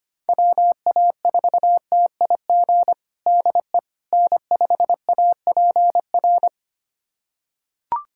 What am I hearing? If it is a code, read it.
Morse code, heard as a keyed tone, about 25 wpm, 700 Hz. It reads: WA4TIG DE N5APR
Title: USA Callsign DE Callsign 25wpm